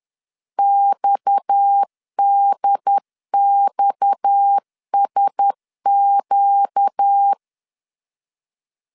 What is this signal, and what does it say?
Morse code, heard as a keyed tone, about 10 wpm, 800 Hz. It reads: XDXSQ
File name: LEZIONI / 6 PKX0 / 6 PKX 10wpm 2426